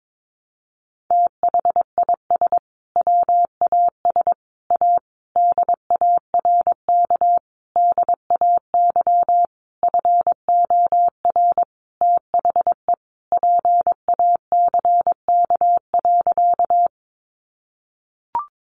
Morse code, heard as a keyed tone, about 22 wpm, 700 Hz. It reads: THIS WAS A DARK DAY FOR THE PACK.
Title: Sentences from Top 1300 Words 22wpm